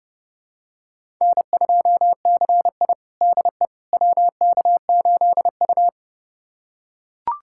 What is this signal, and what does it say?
Morse code, heard as a keyed tone, about 30 wpm, 700 Hz. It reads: N2CI DE WK8U
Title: USA Callsign DE Callsign - 4 Characters - 30wpm